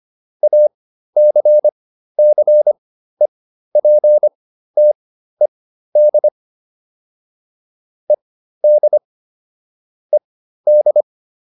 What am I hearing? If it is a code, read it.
Morse code, heard as a keyed tone, about 25 wpm, spaced as for 13 wpm, 600 Hz. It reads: ACCEPTED ED ED